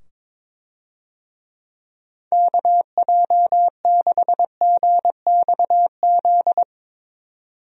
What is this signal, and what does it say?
Morse code, heard as a keyed tone, about 22 wpm, 700 Hz. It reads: KJ6GXZ
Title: USA Call Signs - 6 Characters - 22wpm